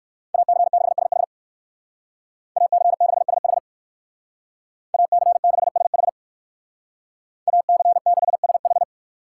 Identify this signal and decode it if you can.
Morse code, heard as a keyed tone, about 58 wpm, 700 Hz. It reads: AC6SH AC6SH AC6SH AC6SH